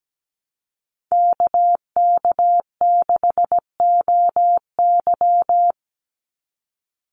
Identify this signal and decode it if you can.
Morse code, heard as a keyed tone, about 17 wpm, 700 Hz. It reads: KK6OY
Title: USA Call Signs 17wpm